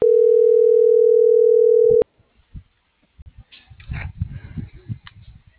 Ringback tone (North America), 440 + 480 Hz.